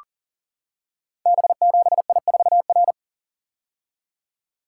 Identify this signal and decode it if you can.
Morse code, heard as a keyed tone, about 38 wpm, 700 Hz. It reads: B7I4R
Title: Code Groups - 5 Letter:Numbers - 40wpm